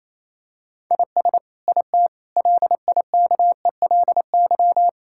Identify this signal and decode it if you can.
Morse code, heard as a keyed tone, about 28 wpm, 700 Hz. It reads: IS IT LIKELY
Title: Sentences from Top 1000 Words - Review Part 5 - 28wpm